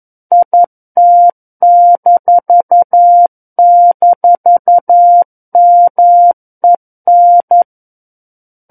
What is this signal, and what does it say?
Morse code, heard as a keyed tone, about 11 wpm, 700 Hz. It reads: IT--MEN
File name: 11 WPM